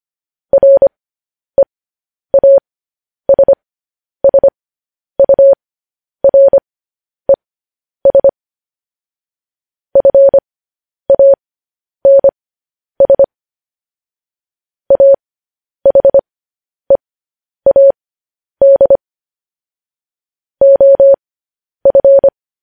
Morse code, heard as a keyed tone, about 25 wpm, spaced as for 10 wpm, 550 Hz. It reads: REASSURES FANS AHEAD OF
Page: News Headlines in Morse Code at 10 WPM